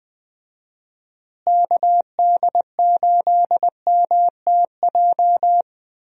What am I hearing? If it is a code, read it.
Morse code, heard as a keyed tone, about 20 wpm, 700 Hz. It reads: KD8MTJ
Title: USA Call Signs - 6 Characters - Repeated 3x - 20wpm